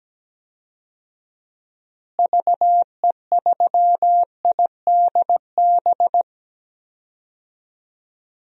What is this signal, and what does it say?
Morse code, heard as a keyed tone, about 17 wpm, 700 Hz. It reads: VE3IDB
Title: DXCC-100 Call Signs - 6 Characters - Repeated 2x - 17wpm